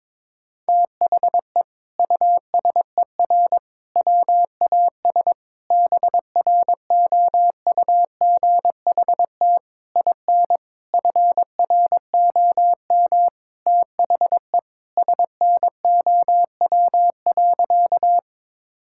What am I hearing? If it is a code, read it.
Morse code, heard as a keyed tone, about 22 wpm, 700 Hz. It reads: THE USER WAS BROUGHT IN FROM THE SNOW.
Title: Sentences from Top 1000 Words - Review Part 5 - 22wpm